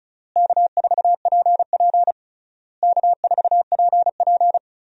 Morse code, heard as a keyed tone, about 36 wpm, 700 Hz. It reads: K4PP K4PP
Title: USA Call Signs - 4 Characters - Repeated 3x - 35wpm